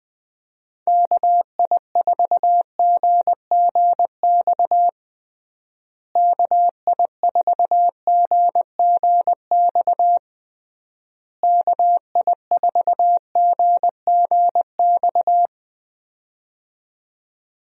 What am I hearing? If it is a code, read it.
Morse code, heard as a keyed tone, about 21 wpm, 700 Hz. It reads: KI4GGX KI4GGX KI4GGX
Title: USA Call Signs - 6 Characters - Repeated 3x - 20wpm